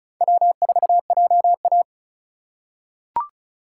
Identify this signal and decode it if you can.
Morse code, heard as a keyed tone, about 36 wpm, 700 Hz. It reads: W4JA
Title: USA Call Signs - 4 Characters - 3x Character Spacing - 35wpm